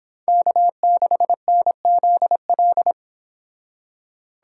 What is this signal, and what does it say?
Morse code, heard as a keyed tone, about 32 wpm, 700 Hz. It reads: K6NZL